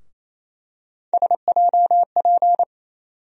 Morse code, heard as a keyed tone, about 28 wpm, 700 Hz. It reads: SJP
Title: USA Call Sign Postfixes - 3 Letters 28wpm